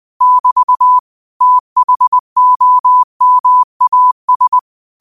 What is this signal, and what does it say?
Morse code, heard as a keyed tone, about 20 wpm, 1000 Hz. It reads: <BT> THOMAS
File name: qotdfiles / 20wpm